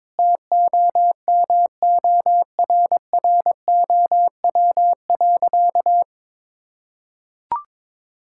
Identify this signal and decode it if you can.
Morse code, heard as a keyed tone, about 22 wpm, 700 Hz. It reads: TOMORROW.
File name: practice / 22wpm / Sentences with 3-Word Phrases - 22wpm